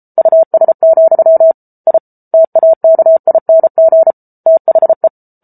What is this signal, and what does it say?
Morse code, heard as a keyed tone, about 34 wpm, 650 Hz. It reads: US, I TAKING THE
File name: Book Two / 07-Chapter 7 - 33 WPM